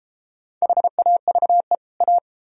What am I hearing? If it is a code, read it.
Morse code, heard as a keyed tone, about 32 wpm, 700 Hz. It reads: HAVE A